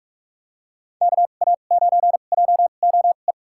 Morse code, heard as a keyed tone, about 46 wpm, spaced as for 33 wpm, 700 Hz. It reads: KA9JOE
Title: USA Call Signs - 2x Character Spacing - 45wpm